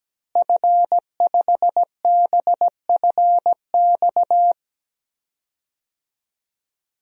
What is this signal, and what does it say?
Morse code, heard as a keyed tone, about 17 wpm, 700 Hz. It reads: F5BFX
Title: DXCC-100 Call Signs - 5 Characters - Repeated 3x - 17wpm